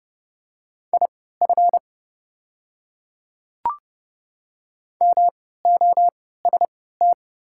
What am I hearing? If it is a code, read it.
Morse code, heard as a keyed tone, about 30 wpm, spaced as for 17 wpm, 700 Hz. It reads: IF MOST
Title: 3x Character Spacing - Top 100 Words - 30wpm